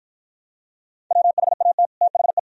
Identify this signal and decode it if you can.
Morse code, heard as a keyed tone, about 48 wpm, 700 Hz. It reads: WHAT THE